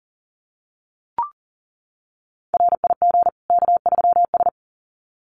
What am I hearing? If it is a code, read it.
Morse code, heard as a keyed tone, about 40 wpm, 700 Hz. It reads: RIG K3S
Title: Common QSO Phrases 40wpm